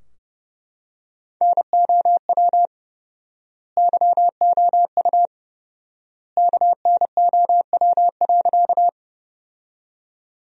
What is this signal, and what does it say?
Morse code, heard as a keyed tone, about 30 wpm, 700 Hz. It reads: NOW YOU KNOW.
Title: 4x Word-Spacing - Sentences from Top 100 Words - 30wpm